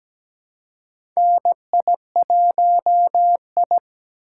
Morse code, heard as a keyed tone, about 17 wpm, 700 Hz. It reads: NI1I
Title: USA Call Signs - 4 Characters - 4x Character Spacing - 17wpm